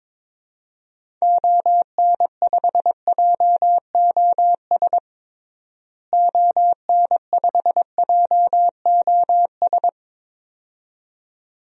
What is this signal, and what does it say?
Morse code, heard as a keyed tone, about 22 wpm, 700 Hz. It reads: ON5JOS ON5JOS
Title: DXCC-100 Call Signs - 6 Characters - Repeated 2x - 22wpm